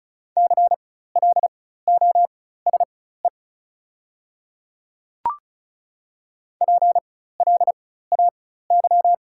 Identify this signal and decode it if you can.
Morse code, heard as a keyed tone, about 36 wpm, spaced as for 16 wpm, 690 Hz. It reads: CLOSE PLAY
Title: 4x Character Spacing - Top 300 Words - 35wpm